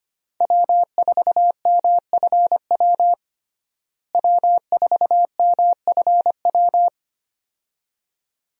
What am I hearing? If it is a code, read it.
Morse code, heard as a keyed tone, about 25 wpm, 700 Hz. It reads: W4MFW W4MFW
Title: USA Call Signs - 5 Characters - Repeated 2x - 25wpm